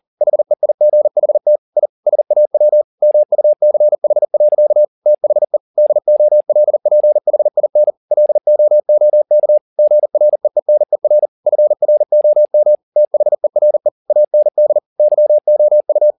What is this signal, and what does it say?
Morse code, heard as a keyed tone, about 40 wpm, 600 Hz. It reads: HEIGHT I SAW MUCH. THE DOLPHIN LOOK GREENER FROM THERE AND YOU
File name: the old man and the sea 2 40wpm